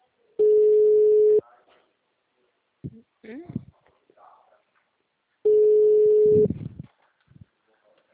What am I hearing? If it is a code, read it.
Ringback tone (Europe, 425 Hz), with speech.